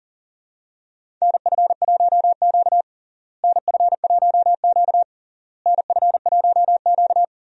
Morse code, heard as a keyed tone, about 40 wpm, 700 Hz. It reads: NF1Q NF1Q NF1Q
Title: USA Call Signs - 4 Characters - Repeated 3x - 40wpm